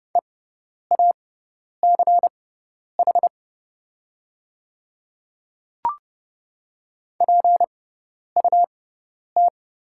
Morse code, heard as a keyed tone, about 30 wpm, spaced as for 10 wpm, 690 Hz. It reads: EACH PUT
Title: 6x Character Spacing - Top 200 Words - 30wpm